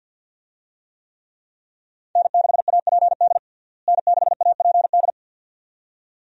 Morse code, heard as a keyed tone, about 48 wpm, 700 Hz. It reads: N6APD N6APD
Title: USA Call Signs - 5 Characters - Repeated 2x - 50wpm